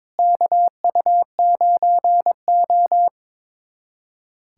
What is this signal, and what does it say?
Morse code, heard as a keyed tone, about 22 wpm, 700 Hz. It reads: KU9O
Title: USA Callsign DE Callsign 22wpm